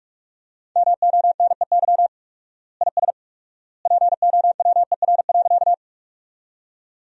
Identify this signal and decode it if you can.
Morse code, heard as a keyed tone, about 44 wpm, 700 Hz. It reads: MONEY IS POWER.
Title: 4x Word-Spacing - Sentences from Top 500 Words - 45wpm